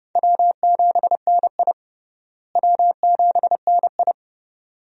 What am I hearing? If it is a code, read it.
Morse code, heard as a keyed tone, about 32 wpm, 700 Hz. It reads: W7NI W7NI
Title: USA Call Signs - 4 Characters - Repeated 2x - 30wpm